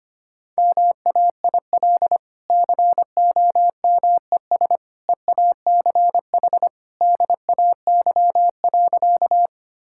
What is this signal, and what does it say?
Morse code, heard as a keyed tone, about 25 wpm, 700 Hz. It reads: MAIL COMES EACH DAY.
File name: Easy Sentences with Hints - 25wpm